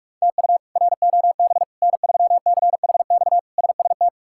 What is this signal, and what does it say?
Morse code, heard as a keyed tone, about 46 wpm, 700 Hz. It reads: TU ROB N3CHX SST
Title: K1USN SST QSO Practice - 45wpm